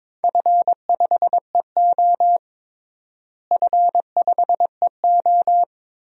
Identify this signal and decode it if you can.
Morse code, heard as a keyed tone, about 21 wpm, 700 Hz. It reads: F5EO F5EO
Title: DXCC-100 Call Signs - 4 Characters - Repeated 2x - 22wpm